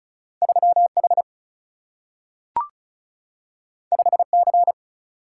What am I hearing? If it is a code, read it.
Morse code, heard as a keyed tone, about 36 wpm, 700 Hz. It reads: 3 H 5 C
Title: Single-Letter Single-Number 35wpm